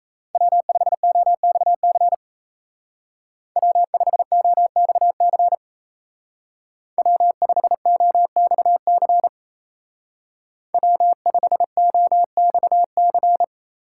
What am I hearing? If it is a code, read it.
Morse code, heard as a keyed tone, about 37 wpm, 700 Hz. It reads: W5OXC W5OXC W5OXC W5OXC